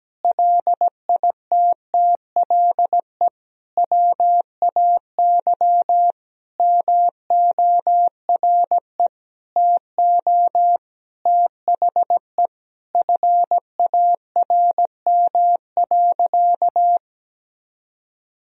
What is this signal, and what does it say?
Morse code, heard as a keyed tone, about 17 wpm, 700 Hz. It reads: LITTLE WAY MORE TO THE FARM.
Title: Sentences from Top 1000 Words - Review Part 2 - 17wpm